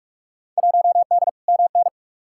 Morse code, heard as a keyed tone, about 46 wpm, 700 Hz. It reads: 1D MN
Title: Field Day Exchanges - 45wpm